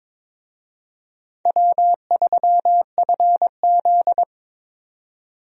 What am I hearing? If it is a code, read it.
Morse code, heard as a keyed tone, about 22 wpm, 700 Hz. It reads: W3FZ
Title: USA Call Signs - 4 Characters - 22wpm